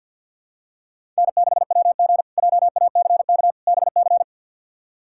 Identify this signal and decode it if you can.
Morse code, heard as a keyed tone, about 52 wpm, 700 Hz. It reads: N6WG JACK BC
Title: K1USN SST QSO Practice - 50wpm